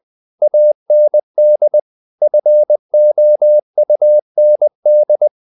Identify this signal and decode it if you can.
Morse code, heard as a keyed tone, about 20 wpm, 600 Hz. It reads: AND FOUND
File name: the old man and the sea / the old man and the sea part 1 20wpm